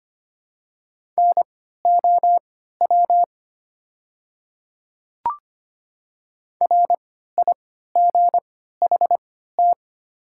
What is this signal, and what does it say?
Morse code, heard as a keyed tone, about 25 wpm, spaced as for 14 wpm, 690 Hz. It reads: NOW RIGHT
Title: 3x Character Spacing - Top 200 Words - 25wpm